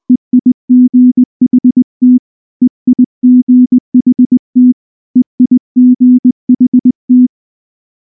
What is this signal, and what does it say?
Morse code, heard as a keyed tone, about 19 wpm, 250 Hz. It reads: EIGHT EIGHT EIGHT
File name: morse / morse-training-20WPM-3